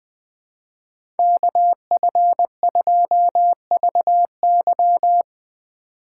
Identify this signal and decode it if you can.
Morse code, heard as a keyed tone, about 20 wpm, 700 Hz. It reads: KF2VY
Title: USA Call Signs - 5 Characters - 2x Character Spacing - 20wpm